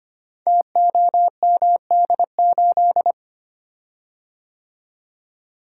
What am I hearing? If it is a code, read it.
Morse code, heard as a keyed tone, about 25 wpm, 700 Hz. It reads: TOMD8
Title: Code Groups - 5 Letter:Numbers - 25wpm